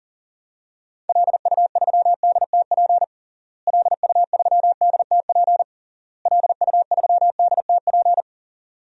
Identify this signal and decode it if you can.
Morse code, heard as a keyed tone, about 40 wpm, 700 Hz. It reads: LU3DTP LU3DTP LU3DTP
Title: DXCC-100 Call Signs - 6 Characters - Repeated 3x - 40wpm